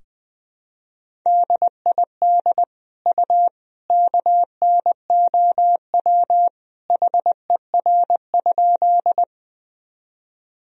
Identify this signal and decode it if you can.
Morse code, heard as a keyed tone, about 20 wpm, 700 Hz. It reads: DID U KNOW HER?
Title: Sentences from Top 100 Words and Abbreviations 20wpm